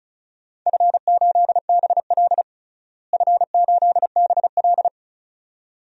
Morse code, heard as a keyed tone, about 34 wpm, 700 Hz. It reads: F8BL F8BL
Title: DXCC-100 Call Signs - 4 Characters - Repeated 2x - 35wpm